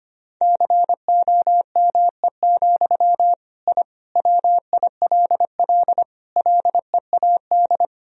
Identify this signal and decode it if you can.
Morse code, heard as a keyed tone, about 25 wpm, 700 Hz. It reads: COME, I WILL LEAD
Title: Sentences from Top 1000 Words - Review Part 2 - 25wpm